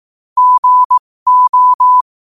Morse code, heard as a keyed tone, about 18 wpm, spaced as for 15 wpm, 1000 Hz. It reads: GO
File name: qotdfiles / 15wpm